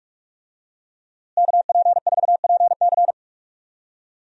Morse code, heard as a keyed tone, about 46 wpm, 700 Hz. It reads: KP4PC
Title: DXCC-100 Call Signs - 5 Characters - Repeated 3x - 45wpm